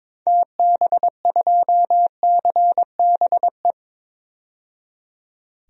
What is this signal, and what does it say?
Morse code, heard as a keyed tone, about 22 wpm, 700 Hz. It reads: TB2CBE
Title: DXCC-100 Call Signs - 6 Characters - 22wpm